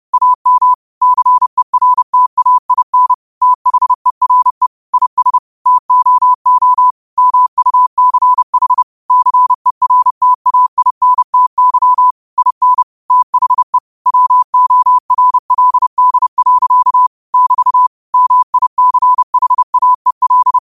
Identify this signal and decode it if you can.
Morse code, heard as a keyed tone, about 31 wpm, 1000 Hz. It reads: AM CERTAIN THERE IS TOO MUCH CERTAINTY IN THE WORLD. <BT> MICHAEL